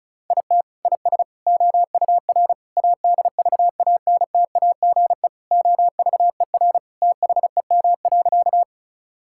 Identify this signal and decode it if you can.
Morse code, heard as a keyed tone, about 36 wpm, 700 Hz. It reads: IT IS OUR ADVANTAGE OVER THEM.
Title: Sentences from Top 1000 Words - Review Part 6 - 35wpm